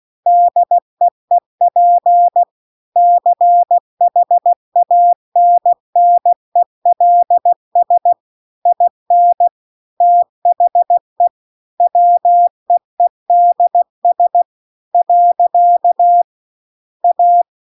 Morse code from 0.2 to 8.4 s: DEEP CHANNELS